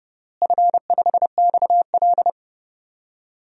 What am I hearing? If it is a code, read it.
Morse code, heard as a keyed tone, about 30 wpm, 700 Hz. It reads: F5XL
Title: DXCC-100 Call Signs - 4 Characters - Repeated 3x - 30wpm